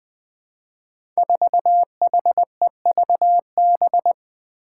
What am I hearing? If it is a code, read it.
Morse code, heard as a keyed tone, about 20 wpm, 700 Hz. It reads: 4HEVB
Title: Five Challenging Letter-Number Combinations 20wpm